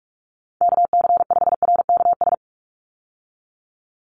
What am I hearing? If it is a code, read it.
Morse code, heard as a keyed tone, about 44 wpm, 700 Hz. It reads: KC5RKS
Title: USA Call Signs 45wpm